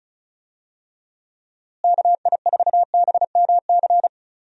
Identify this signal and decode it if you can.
Morse code, heard as a keyed tone, about 34 wpm, 700 Hz. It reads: KI4BMC
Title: USA Call Signs - 6 Characters - Repeated 3x - 35wpm